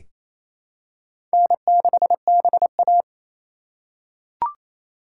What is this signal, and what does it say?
Morse code, heard as a keyed tone, about 28 wpm, 700 Hz. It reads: N6BA
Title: USA Call Signs - 4x Character Spacing - 28wpm